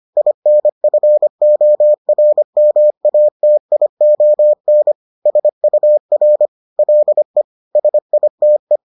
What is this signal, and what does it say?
Morse code, heard as a keyed tone, about 25 wpm, 600 Hz. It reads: INFORMATION SUR LE SITE
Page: bulletin UFT sous forme texte et en CW